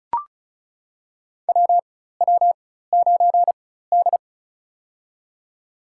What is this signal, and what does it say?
Morse code, heard as a keyed tone, about 36 wpm, spaced as for 16 wpm, 700 Hz. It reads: WW9D
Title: USA Call Signs - 4 Characters - 4x Character Spacing - 35wpm